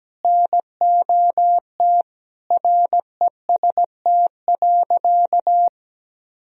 Morse code, heard as a keyed tone, about 17 wpm, 700 Hz. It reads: NOT REST.
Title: Sentences from Top 1000 Words - Review Part 2 - 17wpm